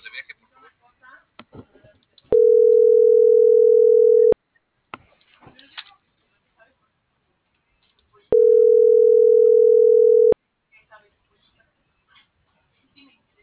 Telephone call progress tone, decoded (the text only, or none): ringback tone